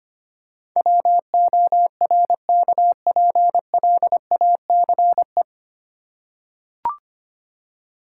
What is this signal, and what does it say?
Morse code, heard as a keyed tone, about 25 wpm, 700 Hz. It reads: WORKPLACE
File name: Nine Letter Words - 25wpm